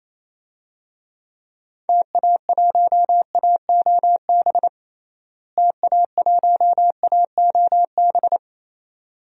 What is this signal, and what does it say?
Morse code, heard as a keyed tone, about 28 wpm, 700 Hz. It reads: TA1AOB TA1AOB
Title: DXCC-100 Call Signs - 6 Characters - Repeated 2x - 28wpm